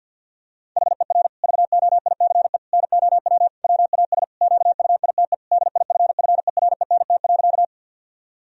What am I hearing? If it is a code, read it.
Morse code, heard as a keyed tone, about 50 wpm, 700 Hz. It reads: HER VOICE NOW WAS QUITE DIFFERENT.
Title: Sentences from Top 1000 Words - Review Part 3 - 50wpm